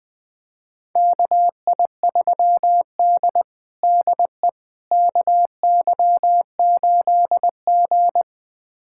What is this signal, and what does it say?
Morse code, heard as a keyed tone, about 20 wpm, 700 Hz. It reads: KI3D DE KY8G
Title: USA Callsign DE Callsign - 4 Characters - 20wpm